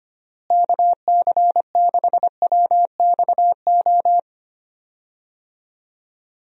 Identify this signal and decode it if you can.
Morse code, heard as a keyed tone, about 25 wpm, 700 Hz. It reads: KC6WXO
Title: USA Call Signs - 6 Characters - Repeated 2x - 25wpm